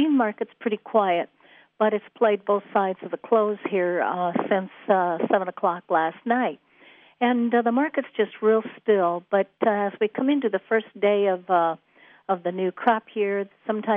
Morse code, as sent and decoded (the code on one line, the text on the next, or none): none